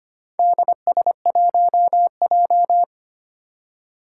Morse code, heard as a keyed tone, about 25 wpm, 700 Hz. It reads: DS1J
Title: Four Challenging Letter-Number Combinations 25wpm